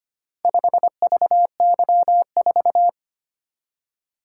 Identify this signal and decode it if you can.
Morse code, heard as a keyed tone, about 25 wpm, 700 Hz. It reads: 5VY4
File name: practice / 25wpm / Four Letter-Number Combinations - 25wpm